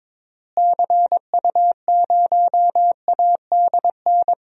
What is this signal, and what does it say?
Morse code, heard as a keyed tone, about 21 wpm, 700 Hz. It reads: CU0ADN
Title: DXCC-100 Call Signs - 6 Characters - Repeated 3x - 22wpm